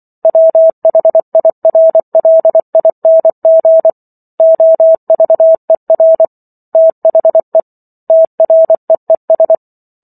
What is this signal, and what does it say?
Morse code, heard as a keyed tone, about 24 wpm, 650 Hz. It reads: WHIRLING OVER THE TREES
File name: Chapter 15 - 24 WPM